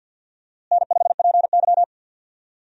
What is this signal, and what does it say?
Morse code, heard as a keyed tone, about 46 wpm, 700 Hz. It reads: N5PY
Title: USA Call Signs - 4 Characters - Repeated 3x - 50wpm